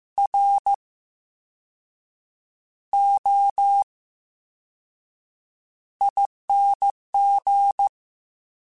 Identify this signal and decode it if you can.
Morse code, heard as a keyed tone, about 15 wpm, 800 Hz. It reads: R O I N G
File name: Special INGs-mp3s / Special INGs-mp3s / VerySlowToFaster Ing @13